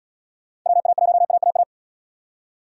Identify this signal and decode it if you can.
Morse code, heard as a keyed tone, about 68 wpm, 700 Hz. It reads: WA1AAA